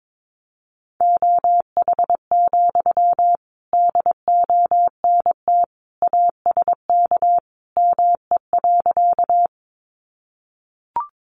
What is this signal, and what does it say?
Morse code, heard as a keyed tone, about 22 wpm, 700 Hz. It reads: OH, DONT ASK ME.
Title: Sentences from Top 500 Words 22wpm